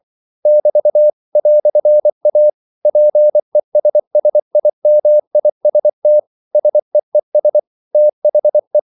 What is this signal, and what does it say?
Morse code, heard as a keyed tone, about 24 wpm, 600 Hz. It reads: <BT> "A PESSIMIST SEES THE
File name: CW / Practice / EnglishQuotations0424